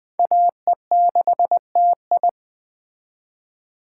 Morse code, heard as a keyed tone, about 21 wpm, 700 Hz. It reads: AE6TI